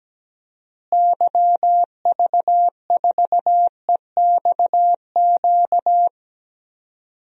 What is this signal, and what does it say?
Morse code, heard as a keyed tone, about 17 wpm, 700 Hz. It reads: YV4EXQ
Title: DXCC-100 Call Signs - 6 Characters - Repeated 2x - 17wpm